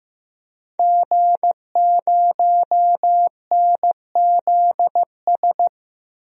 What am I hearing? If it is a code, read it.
Morse code, heard as a keyed tone, about 16 wpm, 700 Hz. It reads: G0NZS
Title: DXCC-100 Call Signs - 5 Characters - 15wpm